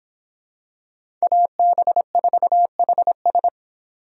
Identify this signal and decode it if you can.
Morse code, heard as a keyed tone, about 23 wpm, 700 Hz. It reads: AB4HS